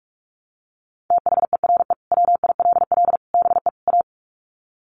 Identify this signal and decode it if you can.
Morse code, heard as a keyed tone, about 44 wpm, 700 Hz. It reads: THERE WILL BE A
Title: Sets of 4 Words 45wpm